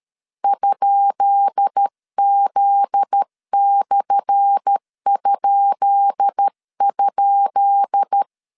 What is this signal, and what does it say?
Morse code, heard as a keyed tone, about 13 wpm, 800 Hz. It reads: ?Z/??
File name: LEZIONI / 10 SIGNS / 10B INT 12wpm 2414